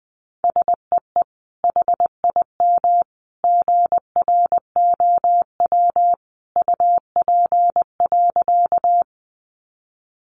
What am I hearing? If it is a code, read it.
Morse code, heard as a keyed tone, about 20 wpm, 700 Hz. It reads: SEE HIM GROW UP.
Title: Sentences from Top 300 Words 20wpm